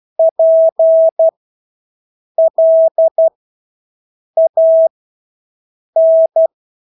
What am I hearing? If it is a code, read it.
Morse code, heard as a keyed tone, about 12 wpm, spaced as for 6 wpm, 650 Hz. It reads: PLAN
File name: Engelske ord 12-6 wpm